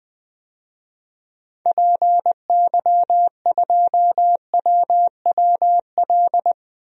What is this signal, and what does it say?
Morse code, heard as a keyed tone, about 20 wpm, 700 Hz. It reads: PY2WWL
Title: DXCC-100 Call Signs - 6 Characters - Repeated 2x - 20wpm